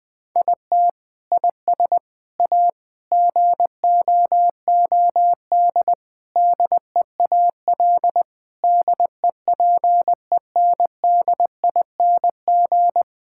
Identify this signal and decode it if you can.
Morse code, heard as a keyed tone, about 21 wpm, 700 Hz. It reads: IT IS A GOOD DEAL DEPENDING
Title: Sentences with 7-Word Phrases - 20wpm